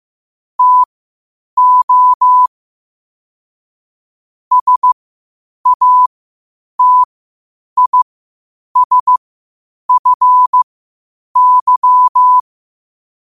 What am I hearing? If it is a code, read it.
Morse code, heard as a keyed tone, about 15 wpm, spaced as for 8 wpm, 1000 Hz. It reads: TO SATISFY
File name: qotdfiles / 8wpm